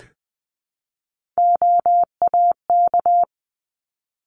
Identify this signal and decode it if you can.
Morse code, heard as a keyed tone, about 20 wpm, 700 Hz. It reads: OAK
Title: 260 Lesson 42e - Words 20-20wpm